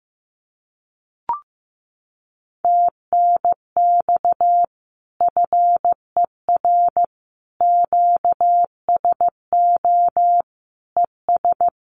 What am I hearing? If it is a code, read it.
Morse code, heard as a keyed tone, about 15 wpm, 700 Hz. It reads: TNX FER QSO ES
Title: Common QSO Phrases Encore 15wpm